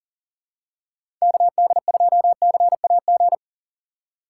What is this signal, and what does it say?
Morse code, heard as a keyed tone, about 40 wpm, 700 Hz. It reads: KD2CAG
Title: USA Call Signs - 6 Characters - 4x Character Spacing - 40wpm